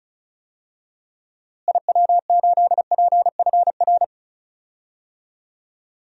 Morse code, heard as a keyed tone, about 36 wpm, 700 Hz. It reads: IW8PFR
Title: DXCC-100 Call Signs - 6 Characters - Repeated 3x - 35wpm